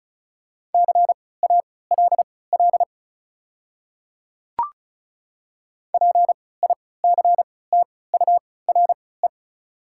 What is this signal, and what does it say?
Morse code, heard as a keyed tone, about 34 wpm, spaced as for 20 wpm, 690 Hz. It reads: CALL PICTURE
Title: 3x Character Spacing - Top 500 Words - 35wpm